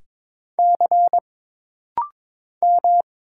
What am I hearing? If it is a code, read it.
Morse code, heard as a keyed tone, about 22 wpm, 700 Hz. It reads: C M
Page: This rapid-fire timing is geared to push your ICR (Instant Character Recognition) skills.